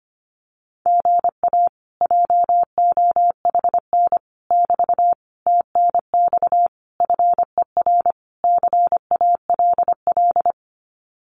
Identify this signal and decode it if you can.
Morse code, heard as a keyed tone, about 25 wpm, 700 Hz. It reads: GA JOHN <BT> TNX FER CALL
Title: Common QSO Phrases Encore 25wpm